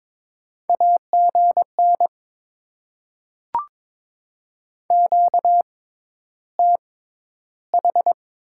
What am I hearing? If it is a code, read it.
Morse code, heard as a keyed tone, about 22 wpm, spaced as for 11 wpm, 690 Hz. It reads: AGN QTH
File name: Top 20 QSO Elements - 6x Character Spacing - 22wpm